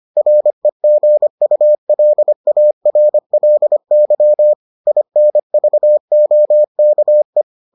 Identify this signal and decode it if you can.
Morse code, heard as a keyed tone, about 25 wpm, 600 Hz. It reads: REGULARLY INVOKE